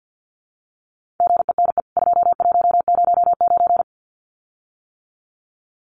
Morse code, heard as a keyed tone, about 49 wpm, 700 Hz. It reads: GENE 2109
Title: CWT Exchange 50wpm